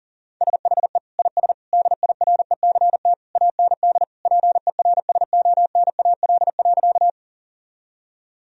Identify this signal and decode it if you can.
Morse code, heard as a keyed tone, about 40 wpm, 700 Hz. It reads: SHE IS DIRECT AND PERSONAL.
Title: Sentences from Top 1000 Words - Review Part 4 - 40wpm